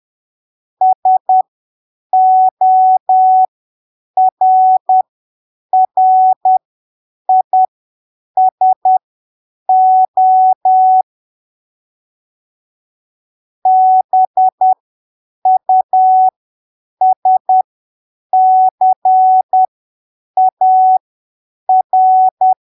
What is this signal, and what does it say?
Morse code, heard as a keyed tone, about 10 wpm, spaced as for 7 wpm, 750 Hz. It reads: SORRISO BUSCAR
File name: arquivos / texto / texto 15 10ppm